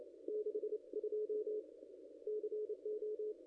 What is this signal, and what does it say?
Morse code, heard as a keyed tone, about 28 wpm, 440 Hz. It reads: 62 CO
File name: contests / ss-cw-k5tr-op-n5rz -0100